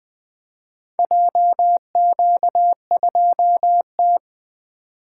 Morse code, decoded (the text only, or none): JQ2T